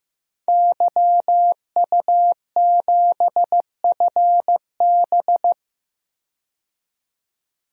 Morse code, heard as a keyed tone, about 16 wpm, 700 Hz. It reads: YU7FB
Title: DXCC-100 Call Signs - 5 Characters - 15wpm